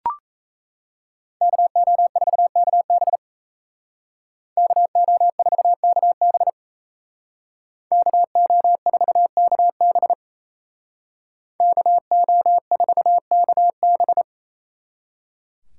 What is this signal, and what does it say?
Morse code, heard as a keyed tone, about 34 wpm, 700 Hz. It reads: KO4KB KO4KB KO4KB KO4KB
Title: Speed Racing - USA 2x2 Call Signs 28wpm